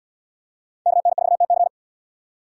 Morse code, heard as a keyed tone, about 78 wpm, 700 Hz. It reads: KN4TC